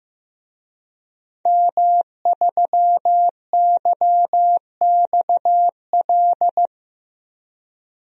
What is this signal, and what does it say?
Morse code, heard as a keyed tone, about 15 wpm, 700 Hz. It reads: M3YXL
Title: DXCC-100 Call Signs - 5 Characters - Repeated 2x - 15wpm